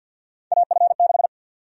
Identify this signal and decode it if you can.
Morse code, heard as a keyed tone, about 51 wpm, 700 Hz. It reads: AF6
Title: USA Call Sign Prefixes - 50wpm